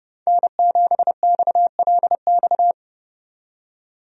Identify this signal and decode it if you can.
Morse code, heard as a keyed tone, about 30 wpm, 700 Hz. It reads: N7XLX
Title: USA Call Signs - 5 Characters - 30wpm